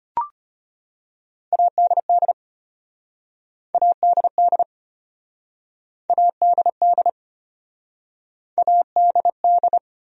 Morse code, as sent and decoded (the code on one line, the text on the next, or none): .- -.. -.. / .- -.. -.. / .- -.. -.. / .- -.. -..
ADD ADD ADD ADD